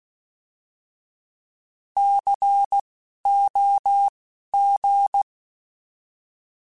Morse code, heard as a keyed tone, about 16 wpm, spaced as for 12 wpm, 800 Hz. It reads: COG
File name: Alphabetical / Alphabetical mp3s / Cs @10